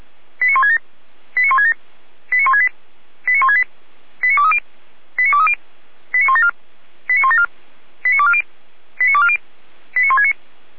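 Five-tone selective call (ZVEI), heard as quick runs of five tones, tones of about 70 ms. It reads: 89167 89167 89168 89168 89230 89230 89174 89174 89240 89240 89169
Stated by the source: ZVEI five-tone selective call received by SDR.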